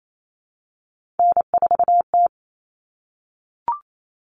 Morse code, heard as a keyed tone, about 28 wpm, 700 Hz. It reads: N4T
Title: All Cut Numbers - 3-Digit 28wpm